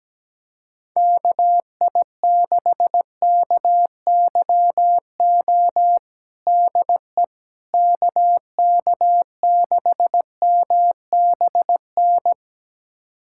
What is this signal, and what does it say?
Morse code, heard as a keyed tone, about 17 wpm, 700 Hz. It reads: KI6KYO DE KK6MBN
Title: USA Callsign DE Callsign - 6 Characters - 17wpm